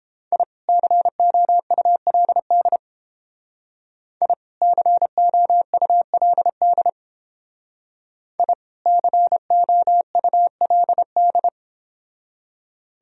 Morse code, heard as a keyed tone, about 30 wpm, 700 Hz. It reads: I COULD I COULD I COULD